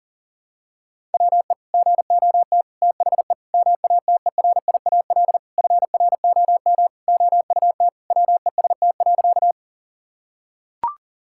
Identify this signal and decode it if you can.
Morse code, heard as a keyed tone, about 40 wpm, 700 Hz. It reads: WE GOT THE MATERIAL FROM OUT WEST.
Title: Sentences from Top 1000 Words - Review Part 5 - 40wpm